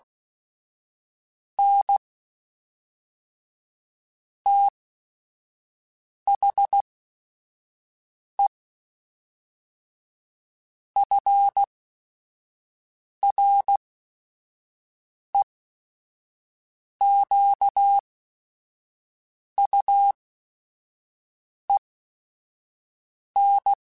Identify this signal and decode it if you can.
Morse code, heard as a keyed tone, about 16 wpm, spaced as for 5 wpm, 800 Hz. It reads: N THE FREQUEN
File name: Technical Subjects / Technical Subjects-mp3s / Sw Pwr Sup @5